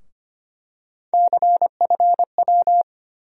Morse code, heard as a keyed tone, about 25 wpm, 700 Hz. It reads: CFW
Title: USA Call Sign Postfixes - 3 Letters 25wpm